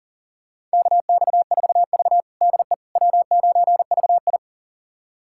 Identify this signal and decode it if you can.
Morse code, heard as a keyed tone, about 40 wpm, 700 Hz. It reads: KX4V DE W9VI
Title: USA Callsign DE Callsign - 4 Characters - 40wpm